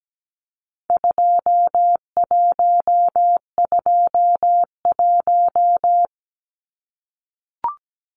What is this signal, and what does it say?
Morse code, heard as a keyed tone, about 17 wpm, 700 Hz. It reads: 2121
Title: Numbers - 4-Digit 17wpm